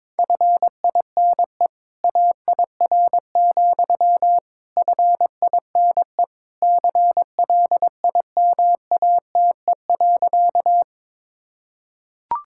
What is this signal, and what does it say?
Morse code, heard as a keyed tone, about 22 wpm, 700 Hz. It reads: FINE AIR, FINE CLIMATE.
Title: Sentences from Top 1600 Words 22wpm